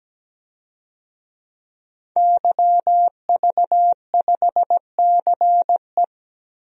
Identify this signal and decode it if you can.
Morse code, heard as a keyed tone, about 17 wpm, 700 Hz. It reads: YV5CE
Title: DXCC-100 Call Signs - 5 Characters - 17wpm